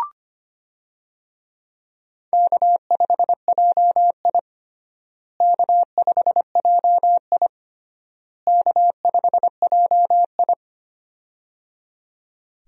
Morse code, heard as a keyed tone, about 25 wpm, 700 Hz. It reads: K5JI K5JI K5JI
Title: USA Call Signs - 4 Characters - Repeated 3x - 25wpm